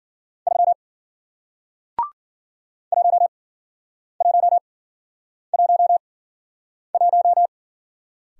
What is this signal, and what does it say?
Morse code, heard as a keyed tone, about 50 wpm, spaced as for 9 wpm, 700 Hz. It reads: 3 1111
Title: Speed-Racing Single Letter-Number 40wpm